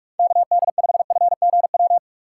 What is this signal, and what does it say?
Morse code, heard as a keyed tone, about 44 wpm, 700 Hz. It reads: KD5FGW
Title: USA Call Signs - 6 Characters - 6x Character Spacing - 45wpm